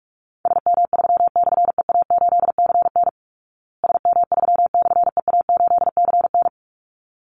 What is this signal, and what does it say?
Morse code, heard as a keyed tone, about 44 wpm, 700 Hz. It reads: SM3/EA8CN SM3/EA8CN
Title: Call Signs Repeated 3x - 45wpm